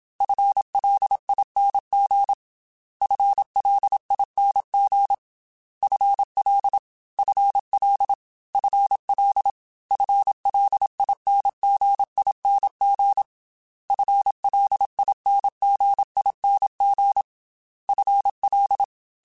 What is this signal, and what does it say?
Morse code, heard as a keyed tone, about 27 wpm, 800 Hz. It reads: FLING FLING FL FL FL FLINGING FLINGING FL
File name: FL / FL mp3s / FL Prefixes 3x2 @25